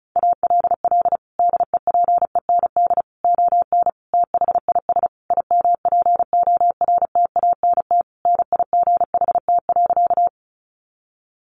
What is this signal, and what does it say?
Morse code, heard as a keyed tone, about 34 wpm, 700 Hz. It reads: ALL DEPEND ON THIS IMPORTANT NIGHT.
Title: Sentences from Top 1300 Words 35wpm